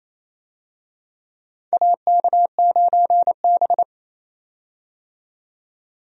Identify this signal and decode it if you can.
Morse code, heard as a keyed tone, about 28 wpm, 700 Hz. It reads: AK9B
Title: USA Call Signs - 4 Characters - 28wpm